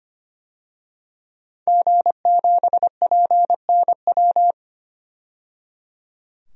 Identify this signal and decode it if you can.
Morse code, heard as a keyed tone, about 25 wpm, 700 Hz. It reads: G7PNW
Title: DXCC-100 Call Signs - 5 Characters - 25wpm